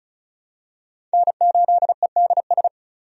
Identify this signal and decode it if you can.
Morse code, heard as a keyed tone, about 36 wpm, 700 Hz. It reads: N8EDS
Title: USA Call Signs - 5 Characters - 35wpm